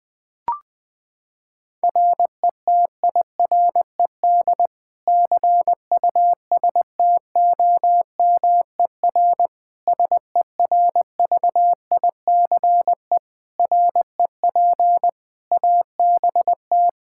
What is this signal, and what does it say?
Morse code, heard as a keyed tone, about 21 wpm, 700 Hz. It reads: RETIRED CUSTOMER SERVICE REP ABT
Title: K1USN SST QSO Practice 20wpm